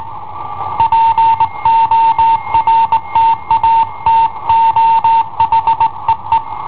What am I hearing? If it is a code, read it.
Morse code, heard as a keyed tone, about 18 wpm, 910 Hz. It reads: PORTATO HEE
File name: i7ohp / cassetta i7ohp B